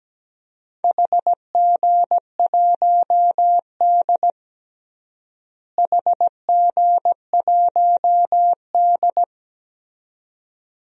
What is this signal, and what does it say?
Morse code, heard as a keyed tone, about 17 wpm, 700 Hz. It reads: HG1D HG1D
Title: DXCC-100 Call Signs - 4 Characters - Repeated 2x - 17wpm